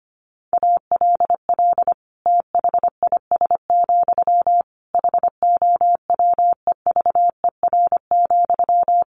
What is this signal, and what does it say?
Morse code, heard as a keyed tone, about 25 wpm, 700 Hz. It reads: ALL THIS, HOWEVER,
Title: Sentences from Top 600 Words 25wpm